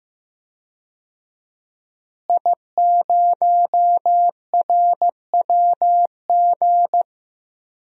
Morse code, heard as a keyed tone, about 15 wpm, 700 Hz. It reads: I0RWG
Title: DXCC-100 Call Signs - 5 Characters - Repeated 2x - 15wpm